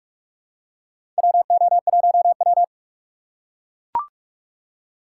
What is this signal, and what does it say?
Morse code, heard as a keyed tone, about 42 wpm, 700 Hz. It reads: WO1W
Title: USA Call Signs - 4 Characters - 8x Character Spacing - 45wpm